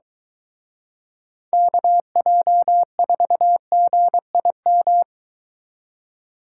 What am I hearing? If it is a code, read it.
Morse code, heard as a keyed tone, about 23 wpm, 700 Hz. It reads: KJ4GIM